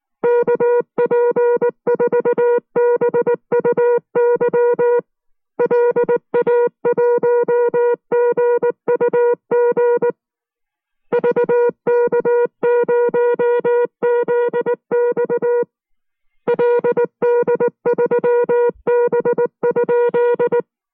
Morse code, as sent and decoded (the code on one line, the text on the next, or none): -.- .--. ....- -... ..- -.-- / .-.. .- .---- --. ..- --. / ...- -.- ----- --.. -..- / .-.. -.. ...-- -... ..--..
KP4BUY LA1GUG VK0ZX LD3B?